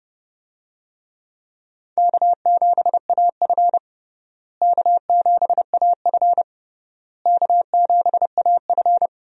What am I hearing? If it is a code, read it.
Morse code, heard as a keyed tone, about 32 wpm, 700 Hz. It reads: K7AF K7AF K7AF
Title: USA Call Signs - 4 Characters - Repeated 3x - 30wpm